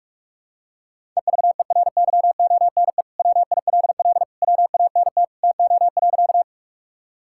Morse code, heard as a keyed tone, about 41 wpm, 700 Hz. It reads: EVERYONE WILL WANT TO.
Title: Sentences from Top 1000 Words - Review Part 5 - 45wpm